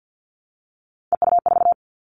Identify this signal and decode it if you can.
Morse code, heard as a keyed tone, about 50 wpm, 700 Hz. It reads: EU4
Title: Three Challenging Letter-Number Combinations 50wpm